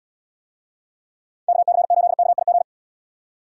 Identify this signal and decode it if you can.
Morse code, heard as a keyed tone, about 77 wpm, 700 Hz. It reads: KC8KIG